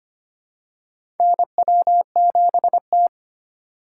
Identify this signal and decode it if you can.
Morse code, heard as a keyed tone, about 25 wpm, 700 Hz. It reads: NW7T
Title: USA Call Signs - 4 Characters - 4x Character Spacing - 25wpm